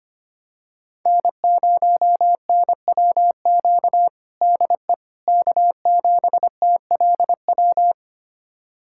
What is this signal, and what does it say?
Morse code, heard as a keyed tone, about 25 wpm, 700 Hz. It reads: N0NWQ DE K7TLW
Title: USA Callsign DE Callsign - 5 Characters - 25wpm